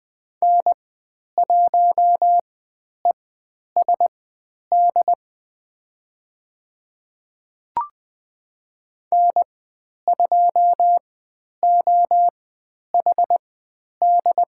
Morse code, heard as a keyed tone, about 20 wpm, spaced as for 10 wpm, 700 Hz. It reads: N1ESD N2OHD
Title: 095 Lesson 16g - Letter c - Call Signs 20wpm